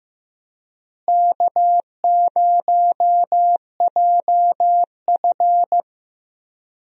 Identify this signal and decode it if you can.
Morse code, heard as a keyed tone, about 15 wpm, 700 Hz. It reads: K0JF
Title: USA Call Signs - 4 Characters - Repeated 2x - 15wpm